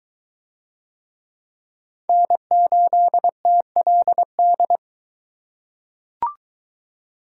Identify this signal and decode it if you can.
Morse code, heard as a keyed tone, about 20 wpm, 700 Hz. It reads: N8TLD